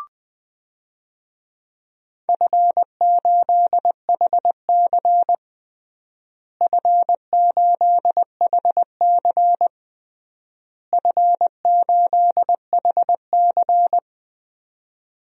Morse code, heard as a keyed tone, about 20 wpm, 700 Hz. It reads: F8HC F8HC F8HC
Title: DXCC-100 Call Signs - 4 Characters - Repeated 3x - 20wpm